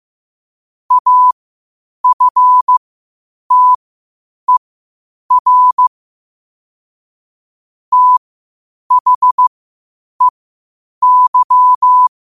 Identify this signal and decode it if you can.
Morse code, heard as a keyed tone, about 15 wpm, spaced as for 8 wpm, 1000 Hz. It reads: AFTER THEY
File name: 8wpm